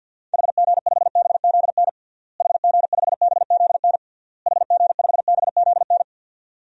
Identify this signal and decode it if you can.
Morse code, heard as a keyed tone, about 51 wpm, 700 Hz. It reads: HG5BZN HG5BZN HG5BZN
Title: DXCC-100 Call Signs - 6 Characters - Repeated 3x - 50wpm